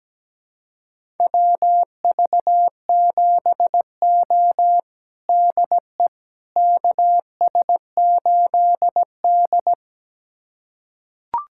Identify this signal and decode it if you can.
Morse code, heard as a keyed tone, about 17 wpm, 700 Hz. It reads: WV7O DE KS8D
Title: USA Callsign DE Callsign - 4 Characters - 17wpm